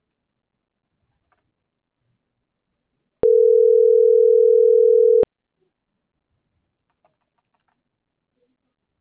Ringback tone (North America), 440 + 480 Hz.